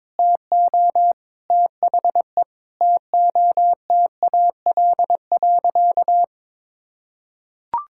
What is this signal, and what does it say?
Morse code, heard as a keyed tone, about 22 wpm, 700 Hz. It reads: TO THE TOTAL.
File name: practice / 22wpm / Sentences with 8-Word Phrases - 22wpm